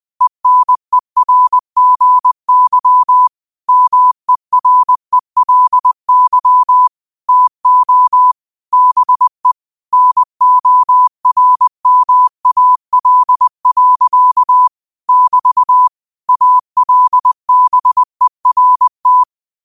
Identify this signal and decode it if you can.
Morse code, heard as a keyed tone, about 20 wpm, 1000 Hz. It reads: ENERGY MERELY TO BE NORMAL. <BT> ALBERT
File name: qotdfiles / 20wpm